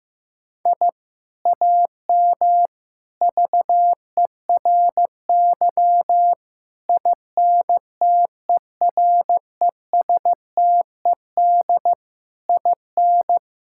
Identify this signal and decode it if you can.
Morse code, heard as a keyed tone, about 15 wpm, 700 Hz. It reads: I AM VERY INTERESTED IN
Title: Sentences from Top 1000 Words - Review Part 6 - 15wpm